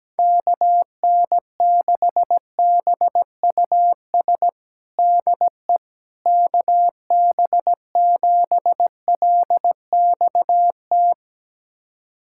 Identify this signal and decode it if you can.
Morse code, heard as a keyed tone, about 17 wpm, 700 Hz. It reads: KN6BUS DE KB7LXT
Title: USA Callsign DE Callsign - 6 Characters - 17wpm